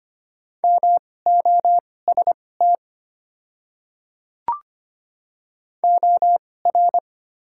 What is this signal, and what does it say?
Morse code, heard as a keyed tone, about 25 wpm, spaced as for 18 wpm, 690 Hz. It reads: MOST OR
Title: 2x Character Spacing - Top 100 Words - 25wpm